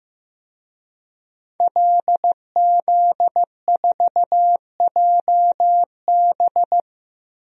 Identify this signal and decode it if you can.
Morse code, heard as a keyed tone, about 15 wpm, 700 Hz. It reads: LZ4JB
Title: DXCC-100 Call Signs - 5 Characters - 15wpm